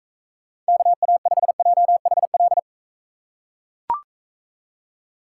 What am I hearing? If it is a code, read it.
Morse code, heard as a keyed tone, about 40 wpm, 700 Hz. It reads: KA5JHL